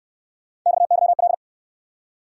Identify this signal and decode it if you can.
Morse code, heard as a keyed tone, about 62 wpm, 700 Hz. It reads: XYL